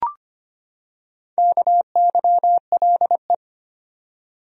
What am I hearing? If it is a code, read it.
Morse code, heard as a keyed tone, about 25 wpm, 700 Hz. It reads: KYLE
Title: USA Common Names - Max 5 Letters - 25wpm